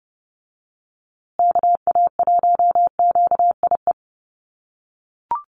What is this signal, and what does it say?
Morse code, heard as a keyed tone, about 30 wpm, 700 Hz. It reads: KA1QIE
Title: USA Call Signs Repeated 3x 30wpm